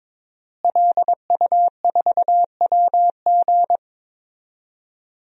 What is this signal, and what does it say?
Morse code, heard as a keyed tone, about 22 wpm, 700 Hz. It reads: LU4WG
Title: DXCC-100 Call Signs - 5 Characters - Repeated 3x - 22wpm